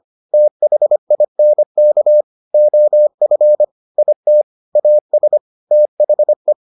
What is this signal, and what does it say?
Morse code, heard as a keyed tone, about 25 wpm, 600 Hz. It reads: THINK OF IT AS THE